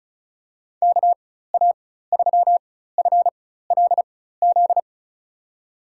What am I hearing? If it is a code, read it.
Morse code, heard as a keyed tone, about 34 wpm, spaced as for 16 wpm, 700 Hz. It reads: KA3FLZ
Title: USA Call Signs - 4x Character Spacing - 35wpm